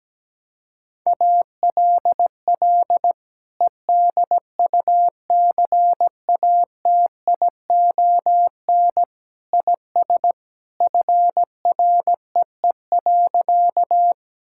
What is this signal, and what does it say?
Morse code, heard as a keyed tone, about 17 wpm, 700 Hz. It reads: ALL EDUCATION IS FREE.
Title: Sentences from Top 1000 Words - Review Part 6 - 17wpm